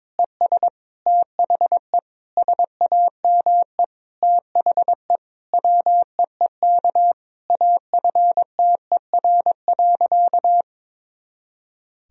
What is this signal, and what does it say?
Morse code, heard as a keyed tone, about 22 wpm, 700 Hz. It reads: ES THE SAME THE WEEK AFTER.
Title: Sentences from Top 500 Words and Abbreviations 22wpm